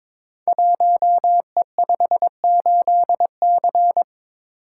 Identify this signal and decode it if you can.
Morse code, heard as a keyed tone, about 23 wpm, 700 Hz. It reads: 1E58C
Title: Code Groups - 5 Letter:Numbers - 22wpm